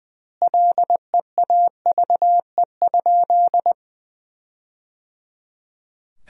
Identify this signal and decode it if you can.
Morse code, heard as a keyed tone, about 21 wpm, 700 Hz. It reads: LEAVE?
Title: 2x Word-Spacing - Sentences from Top 400 Words - 20wpm